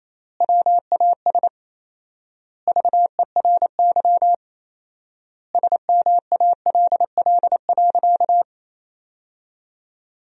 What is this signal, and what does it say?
Morse code, heard as a keyed tone, about 28 wpm, 700 Hz. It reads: WAS VERY SMALL.
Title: 4x Word-Spacing - Sentences from Top 400 Words - 28wpm